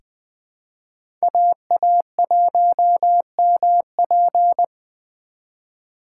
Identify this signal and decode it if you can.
Morse code, heard as a keyed tone, about 20 wpm, 700 Hz. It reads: AA1MP
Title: USA Call Signs - 5 Characters - 2x Character Spacing - 20wpm